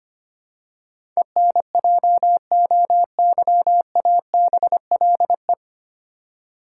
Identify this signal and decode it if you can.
Morse code, heard as a keyed tone, about 25 wpm, 700 Hz. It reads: ENJOYABLE
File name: Nine Letter Words - 25wpm